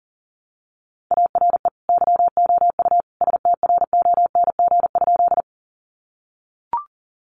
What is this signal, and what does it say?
Morse code, heard as a keyed tone, about 40 wpm, 700 Hz. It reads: ARE YOU STRONG?
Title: Sentences from Top 500 Words 40wpm